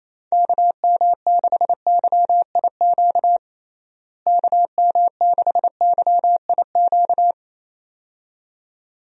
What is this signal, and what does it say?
Morse code, heard as a keyed tone, about 27 wpm, 700 Hz. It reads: KM6YIQ KM6YIQ
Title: USA Call Signs - 6 Characters - Repeated 2x - 28wpm